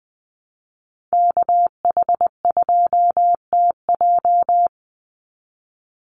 Morse code, heard as a keyed tone, about 20 wpm, 690 Hz. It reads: KH2TJ
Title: NA SOTA Chaser Callsigns 20wpm